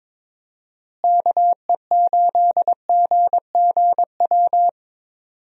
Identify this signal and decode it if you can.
Morse code, heard as a keyed tone, about 22 wpm, 700 Hz. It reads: KE8GGW
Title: USA Call Signs - 6 Characters - Repeated 3x - 22wpm